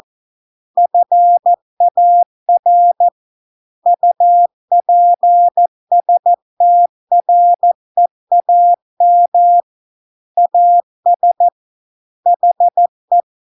Morse code, heard as a keyed tone, about 14 wpm, 700 Hz. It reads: FAR UPSTREAM AS HE